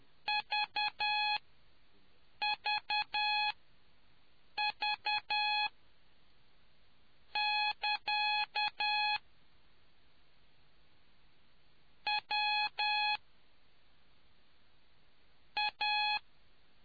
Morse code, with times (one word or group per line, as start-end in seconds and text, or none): 0.3-9.2 VVV<KA>
12.1-13.2 W
15.6-16.2 A